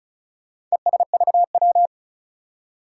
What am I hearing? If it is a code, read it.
Morse code, heard as a keyed tone, about 36 wpm, 700 Hz. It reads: ESVW
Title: Four Challenging Letter-Number Combinations 35wpm